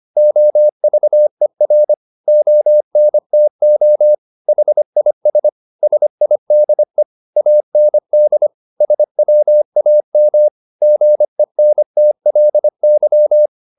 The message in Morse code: OVER ONTO HIS SIDE AND SWAM GENTLY

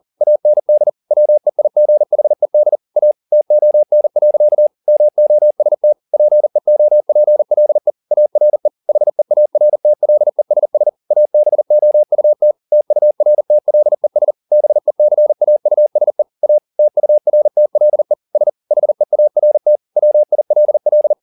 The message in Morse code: AND WEIGHED A TON. MOST PEOPLE ARE HEARTLESS ABOUT TURTLES BECAUSE A TURTLE S HEART WILL

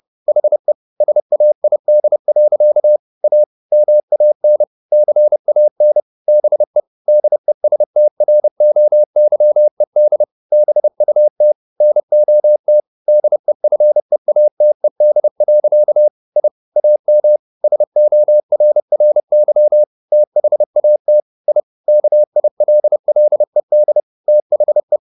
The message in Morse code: HE SAID. A MAN CAN BE DESTROYED BUT NOT DEFEATED. I AM SORRY THAT I KILLED THE

